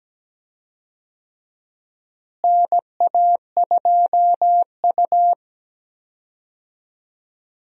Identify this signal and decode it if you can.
Morse code, heard as a keyed tone, about 17 wpm, 700 Hz. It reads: NA2U
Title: USA Call Signs - 4 Characters - Repeated 2x - 17wpm